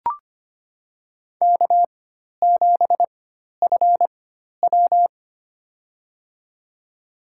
Morse code, heard as a keyed tone, about 25 wpm, spaced as for 12 wpm, 700 Hz. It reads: K7FW
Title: USA Call Signs - 4 Characters - 4x Character Spacing - 25wpm